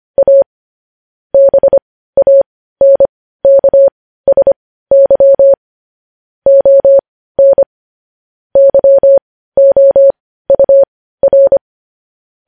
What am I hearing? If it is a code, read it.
Morse code, heard as a keyed tone, about 25 wpm, spaced as for 15 wpm, 550 Hz. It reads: A BANKSY ON YOUR
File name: podcast / News headlines-1x-25x15